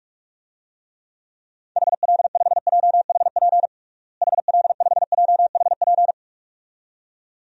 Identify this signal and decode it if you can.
Morse code, heard as a keyed tone, about 41 wpm, 700 Hz. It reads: HL5JHP HL5JHP
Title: DXCC-100 Call Signs - 6 Characters - Repeated 2x - 45wpm